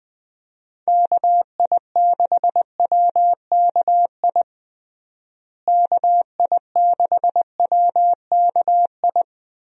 Morse code, heard as a keyed tone, about 20 wpm, 700 Hz. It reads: KI6WKI KI6WKI
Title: USA Call Signs - 6 Characters - Repeated 3x - 20wpm